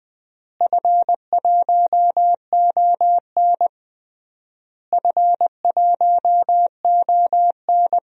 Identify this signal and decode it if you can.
Morse code, heard as a keyed tone, about 20 wpm, 700 Hz. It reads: F1ON F1ON
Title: DXCC-100 Call Signs - 4 Characters - Repeated 3x - 20wpm